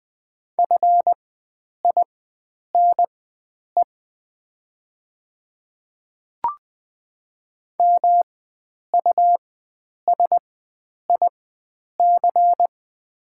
Morse code, heard as a keyed tone, about 20 wpm, spaced as for 9 wpm, 700 Hz. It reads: FINE MUSIC
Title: 4x Character Spacing - Top 500 Words - 20wpm